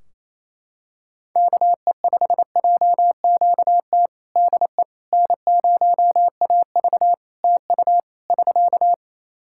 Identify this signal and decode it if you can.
Morse code, heard as a keyed tone, about 28 wpm, 700 Hz. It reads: KE5JQT DE N0AV TU <SK>
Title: Practice QSO - Exchange 4 - Ending - 28wpm